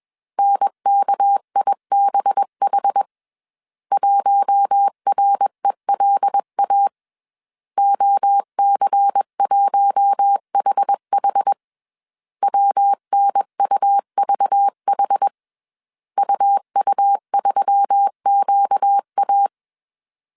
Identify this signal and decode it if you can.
Morse code, heard as a keyed tone, about 21 wpm, 800 Hz. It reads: NKIBH 1RELA OC1HH WNUVH UU3QA
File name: CW / DET / 1 6 1500C S A 20wpm 2048